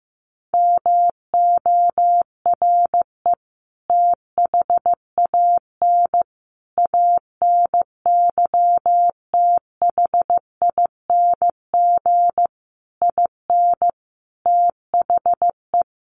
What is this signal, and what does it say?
Morse code, heard as a keyed tone, about 15 wpm, 700 Hz. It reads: MORE THAN ANYTHING IN THE
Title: Sentences from Top 1000 Words 15wpm